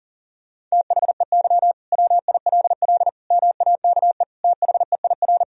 Morse code, heard as a keyed tone, about 40 wpm, 700 Hz. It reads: THEY WILL MAKE THEIR
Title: Sentences from Top 1000 Words - Review Part 3 - 40wpm